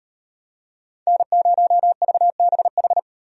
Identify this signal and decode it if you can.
Morse code, heard as a keyed tone, about 36 wpm, 700 Hz. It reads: N0VBH